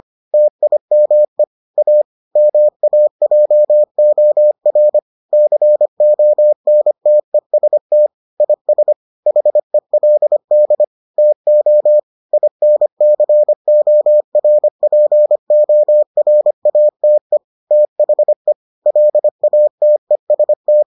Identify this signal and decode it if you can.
Morse code, heard as a keyed tone, about 25 wpm, 600 Hz. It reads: TIME A MAJOR CONTEST IS HELD TO INCORPORATE THE LATEST